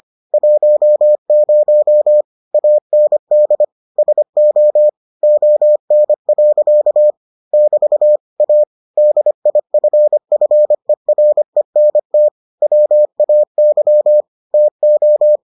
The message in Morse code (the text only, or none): 10 AND SO ON. <BT> A DIFFERENT WAY TO